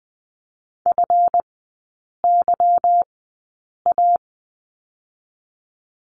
Morse code, heard as a keyed tone, about 20 wpm, spaced as for 8 wpm, 690 Hz. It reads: FYA
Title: 145 Lesson 24e - Letter y - Three Characters 20wpm